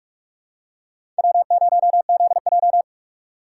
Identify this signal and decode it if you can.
Morse code, heard as a keyed tone, about 42 wpm, 700 Hz. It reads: W0ZJ
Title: USA Call Signs - 4 Characters - 4x Character Spacing - 45wpm